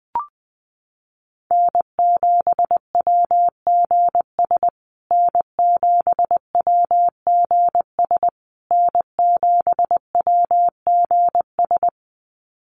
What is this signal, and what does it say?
Morse code, heard as a keyed tone, about 20 wpm, 700 Hz. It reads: N7WGS N7WGS N7WGS
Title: USA Call Signs Repeated 3x 20wpm